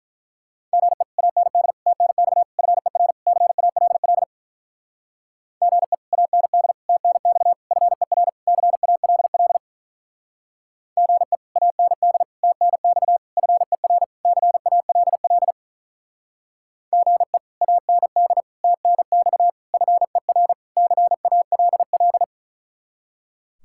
Morse code, heard as a keyed tone, about 43 wpm, 700 Hz. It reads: GE AND TNX FER CALL GE AND TNX FER CALL GE AND TNX FER CALL GE AND TNX FER CALL
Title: Speed-Racing Common QSO Phrases Encore - 35wpm